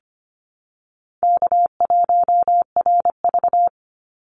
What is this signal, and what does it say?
Morse code, heard as a keyed tone, about 25 wpm, 700 Hz. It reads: K1RV
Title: Easy Call Signs Repeated 2x - 25wpm